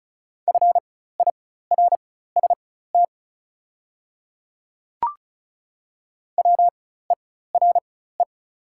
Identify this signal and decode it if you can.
Morse code, heard as a keyed tone, about 35 wpm, spaced as for 16 wpm, 690 Hz. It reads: FIRST WERE
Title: 4x Character Spacing - Top 500 Words - 35wpm